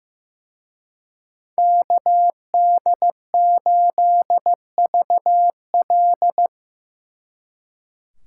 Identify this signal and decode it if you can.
Morse code, heard as a keyed tone, about 18 wpm, 700 Hz. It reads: KD8VL
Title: Speed Racing - USA 2x2 Call Signs 15wpm